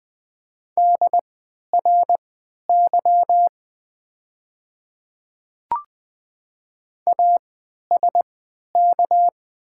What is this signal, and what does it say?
Morse code, heard as a keyed tone, about 20 wpm, spaced as for 11 wpm, 700 Hz. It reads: DRY ASK
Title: 3x Character Spacing - Top 500 Words - 20wpm